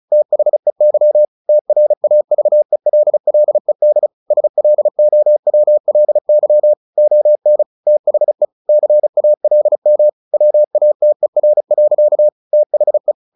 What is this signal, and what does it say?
Morse code, heard as a keyed tone, about 34 wpm, 600 Hz. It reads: THEY TRAVELLED SLOWLY ON THE CALM WATER. THE
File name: the old man and the sea / the old man and the sea 1 35wpm